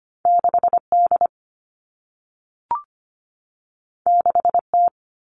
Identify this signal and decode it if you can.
Morse code, heard as a keyed tone, about 25 wpm, 700 Hz. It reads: 6D 6T
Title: All Cut Numbers - 2-Digit 25wpm